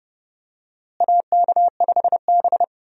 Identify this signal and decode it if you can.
Morse code, heard as a keyed tone, about 30 wpm, 700 Hz. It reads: AK5B
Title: USA Call Signs - 4 Characters - Repeated 2x - 30wpm